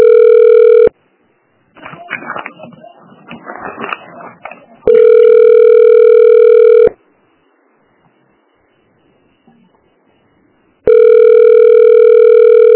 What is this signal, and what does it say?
Ringback tone (North America), 440 + 480 Hz.